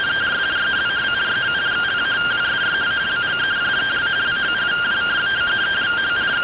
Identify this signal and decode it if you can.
Two-tone teleprinter signal, about 170 Hz shift.
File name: findings / 202 coronal iodine 3mm.gen-09 rtty rx